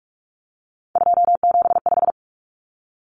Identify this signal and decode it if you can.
Morse code, heard as a keyed tone, about 44 wpm, 700 Hz. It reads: 275
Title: Common Cut Numbers - 3-Digit 45wpm